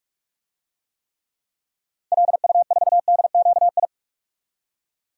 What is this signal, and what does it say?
Morse code, heard as a keyed tone, about 45 wpm, 700 Hz. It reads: LU4DQI
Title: DXCC-100 Call Signs - 6 Characters - 45wpm